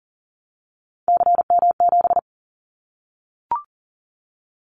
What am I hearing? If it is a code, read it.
Morse code, heard as a keyed tone, about 40 wpm, 700 Hz. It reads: CM7
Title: Callsign Prefixes for Top 100 DXCC - 40wpm